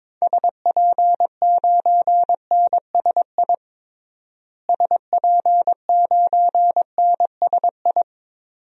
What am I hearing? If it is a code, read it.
Morse code, heard as a keyed tone, about 22 wpm, 700 Hz. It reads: SP9NSI SP9NSI
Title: DXCC-100 Call Signs - 6 Characters - Repeated 2x - 22wpm